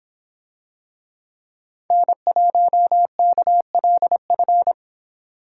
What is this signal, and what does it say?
Morse code, heard as a keyed tone, about 26 wpm, 700 Hz. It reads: N1KLF